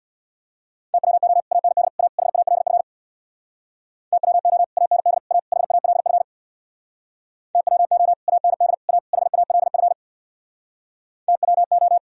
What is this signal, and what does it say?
Morse code, heard as a keyed tone, about 64 wpm, 700 Hz. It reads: TWO AND A HALF TWO AND A HALF TWO AND A HALF TWO